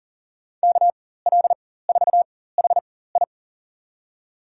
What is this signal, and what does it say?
Morse code, heard as a keyed tone, about 40 wpm, spaced as for 19 wpm, 700 Hz. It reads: KL4HI
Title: USA Call Signs - 4x Character Spacing - 40wpm